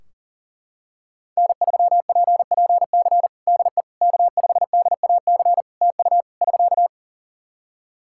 Morse code, heard as a keyed tone, about 40 wpm, 700 Hz. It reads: N3PPC DE K5DAC TU <SK>
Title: K1USN SST QSO Practice 40wpm